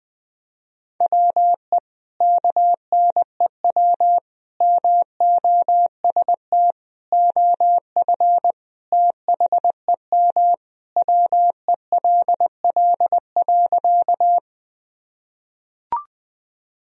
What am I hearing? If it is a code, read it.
Morse code, heard as a keyed tone, about 20 wpm, 700 Hz. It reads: WE KNEW MOST OF THEM WELL.
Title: Sentences from Top 1000 Words - Review Part 1 - 20wpm